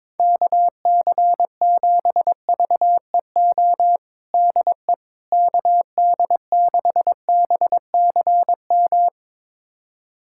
Morse code, heard as a keyed tone, about 22 wpm, 700 Hz. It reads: KC7VEO DE KD6BCM
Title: USA Callsign DE Callsign - 6 Characters - 22wpm